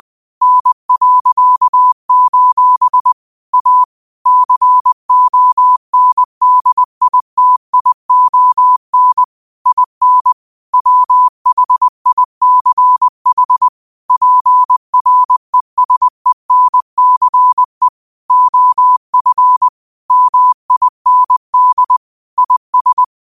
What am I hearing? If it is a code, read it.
Morse code, heard as a keyed tone, about 20 wpm, 1000 Hz. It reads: N.: A CONDITION IN WHICH PRESENCE OF MIND IS
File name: qotdfiles / 20wpm